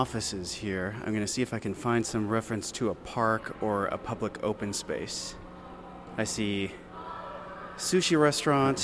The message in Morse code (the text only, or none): none